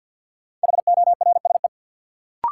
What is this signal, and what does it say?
Morse code, heard as a keyed tone, about 53 wpm, 700 Hz. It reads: HORSE
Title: Sentences from Top 300 Words - New Word Drill - 50wpm